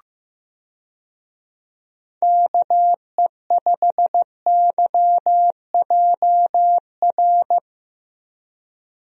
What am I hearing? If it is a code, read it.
Morse code, heard as a keyed tone, about 15 wpm, 700 Hz. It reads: KE5YJR
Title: USA Call Signs - 6 Characters - Repeated 3x - 15wpm